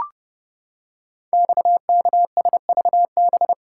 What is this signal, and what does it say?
Morse code, heard as a keyed tone, about 30 wpm, 690 Hz. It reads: XKSVB
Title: Code Groups - 5 Letters - 30wpm